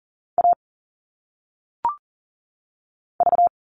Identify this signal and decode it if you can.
Morse code, heard as a keyed tone, about 38 wpm, 700 Hz. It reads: A V
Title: All Cut Numbers - 1-Digit 40wpm